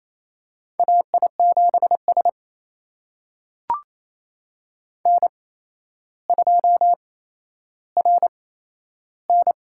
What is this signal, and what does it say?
Morse code, heard as a keyed tone, about 28 wpm, spaced as for 12 wpm, 700 Hz. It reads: AI7S N2RN
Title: USA Call Signs - 8x Character Spacing - 28wpm